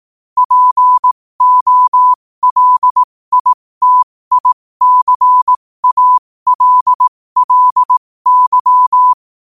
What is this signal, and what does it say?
Morse code, heard as a keyed tone, about 18 wpm, spaced as for 15 wpm, 1000 Hz. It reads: POLITICALLY